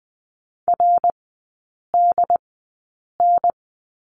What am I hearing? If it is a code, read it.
Morse code, heard as a keyed tone, about 20 wpm, spaced as for 8 wpm, 700 Hz. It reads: RDN
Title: 067 Lesson 12e - Letter d - Three Characters 20wpm